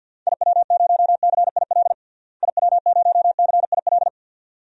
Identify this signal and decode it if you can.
Morse code, heard as a keyed tone, about 48 wpm, 700 Hz. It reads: IW0CIL IW0CIL
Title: DXCC-100 Call Signs - 6 Characters - Repeated 2x - 50wpm